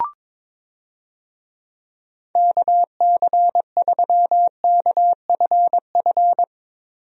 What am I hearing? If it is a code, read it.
Morse code, heard as a keyed tone, about 22 wpm, 700 Hz. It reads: KC3KFF
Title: USA Call Signs - 6 Characters - Repeated 2x - 22wpm